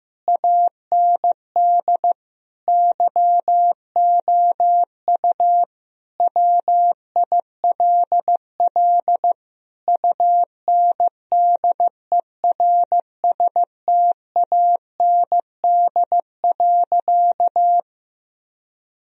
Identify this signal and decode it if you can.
Morse code, heard as a keyed tone, about 16 wpm, 700 Hz. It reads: AND YOU WILL UNDERSTAND.
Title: Sentences from Top 1000 Words - Review Part 1 - 15wpm